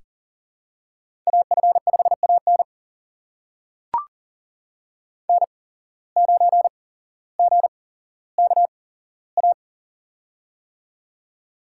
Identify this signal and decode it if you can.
Morse code, heard as a keyed tone, about 38 wpm, spaced as for 17 wpm, 700 Hz. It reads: AF5AN N9GKA
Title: USA Call Signs - 5 Characters - 8x Character Spacing - 40wpm